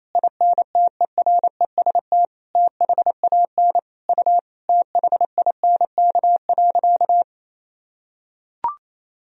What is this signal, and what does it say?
Morse code, heard as a keyed tone, about 28 wpm, 700 Hz. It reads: INTEREST THAN U THINK.
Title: Sentences from Top 400 Words and Abbreviations 28wpm